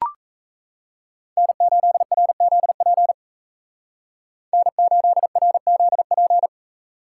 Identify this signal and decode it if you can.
Morse code, heard as a keyed tone, about 35 wpm, 700 Hz. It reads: N8RZP N8RZP
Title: Speed Racing - USA 1x3 Call Signs 28wpm